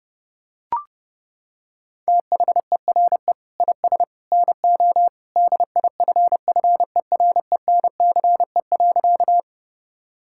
Morse code, heard as a keyed tone, about 30 wpm, 700 Hz. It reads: THERE IS NO DIFFERENCE.
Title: Sentences from Top 1000 Words - Review Part 5 - 30wpm